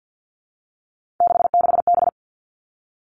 Morse code, heard as a keyed tone, about 50 wpm, 700 Hz. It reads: 66B
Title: Three Challenging Letter-Number Combinations 50wpm